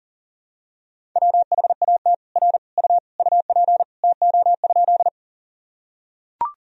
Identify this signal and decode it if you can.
Morse code, heard as a keyed tone, about 37 wpm, 700 Hz. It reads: WHAT R U UP TO?
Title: Sentences from Top 100 Words and Abbreviations 40wpm